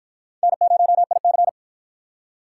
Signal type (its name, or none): Morse code